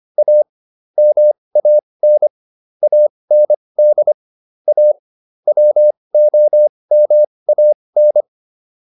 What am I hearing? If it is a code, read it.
Morse code, heard as a keyed tone, about 25 wpm, spaced as for 20 wpm, 600 Hz. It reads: A MAN AND A WOMAN
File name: WD405 20